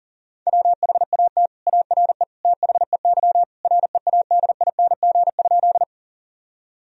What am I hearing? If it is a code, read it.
Morse code, heard as a keyed tone, about 37 wpm, 700 Hz. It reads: WHAT ARE THEY READING?
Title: Sentences from Top 1000 Words - Review Part 3 - 40wpm